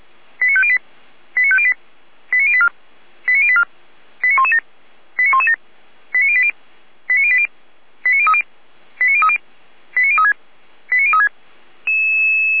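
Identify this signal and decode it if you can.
Five-tone selective call (ZVEI), heard as quick runs of five tones, tones of about 70 ms. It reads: 89508 89508 89084 89084 89117 89117 89080 89080 89030 89030 89036 89036
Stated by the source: ZVEI selective call received on an SDR.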